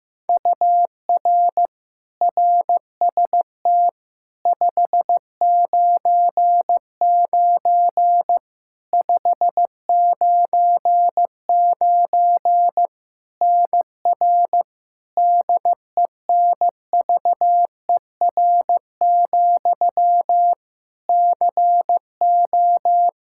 Morse code, heard as a keyed tone, about 15 wpm, 700 Hz. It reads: UR RST 599 599 NR DENVER, CO
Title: Practice QSO - Exchange 1 - Beginning - 15wpm